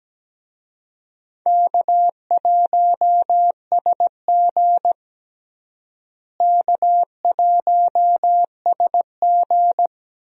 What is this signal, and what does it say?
Morse code, heard as a keyed tone, about 17 wpm, 700 Hz. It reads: K1SG K1SG
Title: USA Call Signs - 4 Characters - Repeated 2x - 17wpm